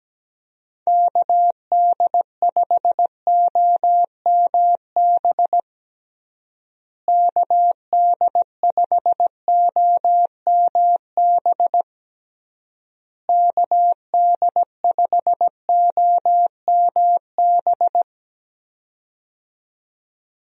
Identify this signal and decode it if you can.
Morse code, heard as a keyed tone, about 17 wpm, 700 Hz. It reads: KD5OMB KD5OMB KD5OMB
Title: USA Call Signs - 6 Characters - Repeated 3x - 17wpm